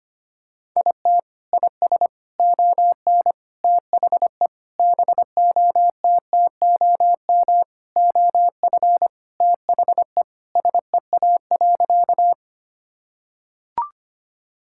Morse code, heard as a keyed tone, about 25 wpm, 700 Hz. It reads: IT IS ON THE BOTTOM OF THE SEA.
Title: Sentences from Top 1000 Words - Review Part 6 - 25wpm